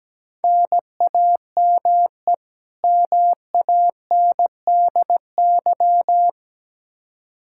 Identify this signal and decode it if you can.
Morse code, heard as a keyed tone, about 17 wpm, 700 Hz. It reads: NAME MANDY
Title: Practice QSO - Exchange 1 - Beginning - 17wpm